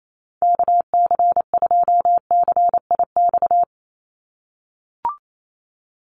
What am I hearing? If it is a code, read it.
Morse code, heard as a keyed tone, about 28 wpm, 700 Hz. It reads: KC2CIX
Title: USA Call Signs 28wpm